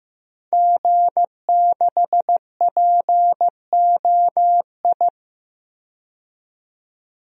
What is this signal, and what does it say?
Morse code, heard as a keyed tone, about 16 wpm, 700 Hz. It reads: G6POI
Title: DXCC-100 Call Signs - 5 Characters - 15wpm